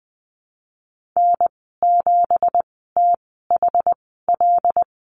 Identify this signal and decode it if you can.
Morse code, heard as a keyed tone, about 19 wpm, spaced as for 14 wpm, 700 Hz. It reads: N7THL
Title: 167 Lesson 27g - Number 7 - Call Signs 20wpm